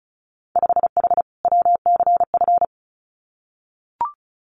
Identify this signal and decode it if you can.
Morse code, heard as a keyed tone, about 34 wpm, 700 Hz. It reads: 5H WCF
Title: Winter Field Day Exchanges Encore 35wpm